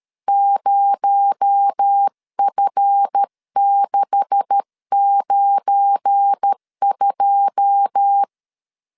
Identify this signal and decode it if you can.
Morse code, heard as a keyed tone, about 13 wpm, 800 Hz. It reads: 0F692
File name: LEZIONI / 9 286 12wpm 2305